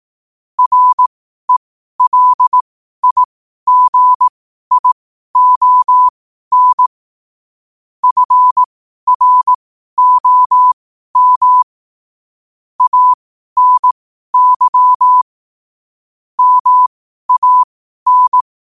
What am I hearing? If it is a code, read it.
Morse code, heard as a keyed tone, about 18 wpm, spaced as for 12 wpm, 1000 Hz. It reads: RELIGION FROM ANY MAN
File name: qotdfiles / 12wpm